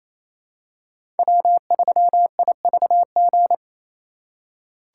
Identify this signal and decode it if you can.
Morse code, heard as a keyed tone, about 28 wpm, 700 Hz. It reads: W3IVG
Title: USA Call Signs - 5 Characters - 4x Character Spacing - 28wpm